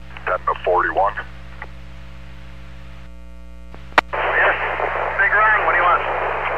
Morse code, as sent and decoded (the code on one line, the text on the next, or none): none